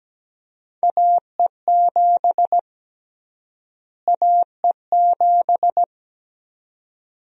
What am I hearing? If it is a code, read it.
Morse code, heard as a keyed tone, about 17 wpm, 700 Hz. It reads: AE7 AE7
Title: USA Call Sign Prefixes - Repeated 2x - 17wpm